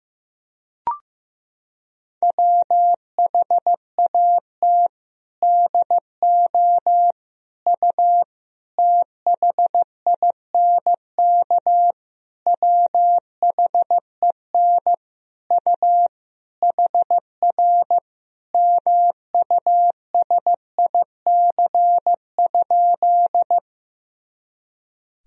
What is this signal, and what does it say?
Morse code, heard as a keyed tone, about 15 wpm, 700 Hz. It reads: WHAT DO U THINK WHEN U HR MUSIC?
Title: Sentences from Top 300 Words and Abbreviations 15wpm